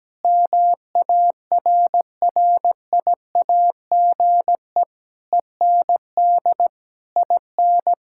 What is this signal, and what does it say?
Morse code, heard as a keyed tone, about 17 wpm, 700 Hz. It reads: MARRIAGE END IN
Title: Sentences from Top 1000 Words - Review Part 6 - 17wpm